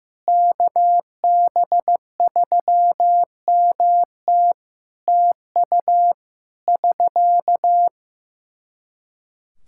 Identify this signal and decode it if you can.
Morse code, heard as a keyed tone, about 16 wpm, 700 Hz. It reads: KB3MT TU <SK>
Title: Practice QSO - Exchange 4 - Ending - 15wpm